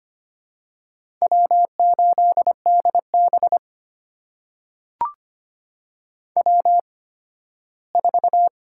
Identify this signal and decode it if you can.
Morse code, heard as a keyed tone, about 25 wpm, spaced as for 15 wpm, 700 Hz. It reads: W8DB W4
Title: USA Call Signs - 4 Characters - 8x Character Spacing - 25wpm